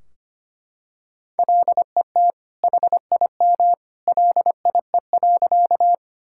Morse code, heard as a keyed tone, about 25 wpm, 700 Hz. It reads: LET HIM LIE.
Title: Sentences from Top 1600 Words 25wpm